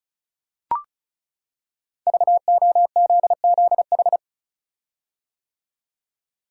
Morse code, heard as a keyed tone, about 34 wpm, 700 Hz. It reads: VOZZH
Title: Code Groups - 5 Letter:Numbers - 35wpm